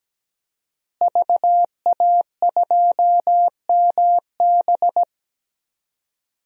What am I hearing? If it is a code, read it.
Morse code, heard as a keyed tone, about 17 wpm, 700 Hz. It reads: VA2MB
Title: DXCC-100 Call Signs - 5 Characters - Repeated 3x - 17wpm